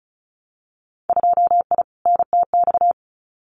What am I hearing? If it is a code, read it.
Morse code, heard as a keyed tone, about 34 wpm, 700 Hz. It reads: 2I NTX
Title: Winter Field Day Exchanges Encore 35wpm